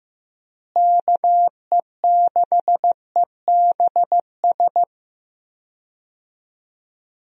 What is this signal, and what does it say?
Morse code, heard as a keyed tone, about 15 wpm, 700 Hz. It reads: KE6EBS
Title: USA Call Signs - 6 Characters - Repeated 3x - 15wpm